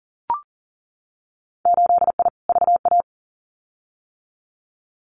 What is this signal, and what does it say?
Morse code, heard as a keyed tone, about 40 wpm, 700 Hz. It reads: 8I VA
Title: Winter Field Day Exchanges Encore 40wpm